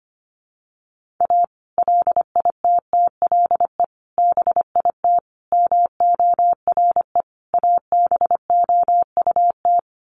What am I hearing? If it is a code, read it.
Morse code, heard as a keyed tone, about 25 wpm, 700 Hz. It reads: A LITTLE BIT MORE ABOUT
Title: Sets of 5 Words Encore2 25wpm